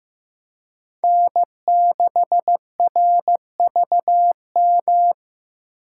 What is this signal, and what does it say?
Morse code, heard as a keyed tone, about 15 wpm, 700 Hz. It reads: N6RVM
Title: USA Call Signs - 2x Character Spacing - 15wpm